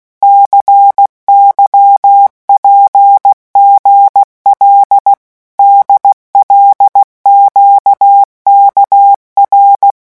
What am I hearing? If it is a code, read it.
Morse code, heard as a keyed tone, about 16 wpm, 800 Hz. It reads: CYPGL DLQKR
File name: morse mp3 / revision5-vitesse 15 mots